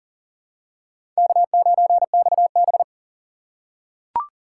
Morse code, heard as a keyed tone, about 40 wpm, 700 Hz. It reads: K9XB
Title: USA Call Signs - 2x Character Spacing - 40wpm